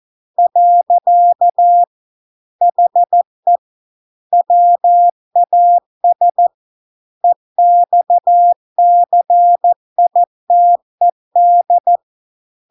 Morse code, heard as a keyed tone, about 14 wpm, 700 Hz. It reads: . HE WAS EXCITED